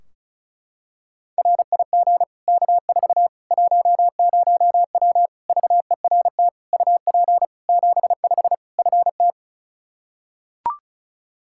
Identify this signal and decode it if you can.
Morse code, heard as a keyed tone, about 36 wpm, 700 Hz. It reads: RIG K4 10W VERT UP 75 FT
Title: Practice QSO - Exchange 2 - Info - 35wpm